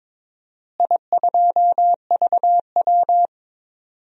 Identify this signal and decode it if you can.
Morse code, heard as a keyed tone, about 22 wpm, 700 Hz. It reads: I2VW
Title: DXCC-100 Call Signs - 4 Characters - Repeated 3x - 22wpm